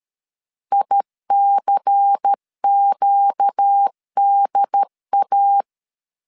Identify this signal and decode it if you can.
Morse code, heard as a keyed tone, about 13 wpm, 800 Hz. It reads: ICQDA
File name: LEZIONI / 7 GF9Z / 7 GF9 12wpm 2319